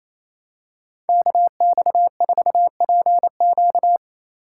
Morse code, heard as a keyed tone, about 28 wpm, 700 Hz. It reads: KX4PQ
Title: K1USN SST QSO Practice - 28wpm